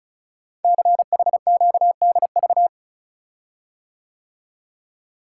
Morse code, heard as a keyed tone, about 36 wpm, 700 Hz. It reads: CHQDV
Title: Code Groups - 5 Letter:Numbers - 35wpm